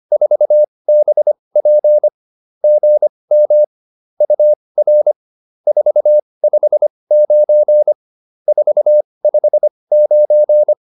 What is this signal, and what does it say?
Morse code, heard as a keyed tone, about 25 wpm, spaced as for 20 wpm, 600 Hz. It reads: 4BP GM UR 459 459